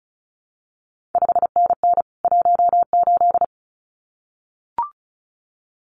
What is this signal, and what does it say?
Morse code, heard as a keyed tone, about 36 wpm, 700 Hz. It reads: 5NN 18
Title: CQ WW DX Contest Exchange Encore - 35wpm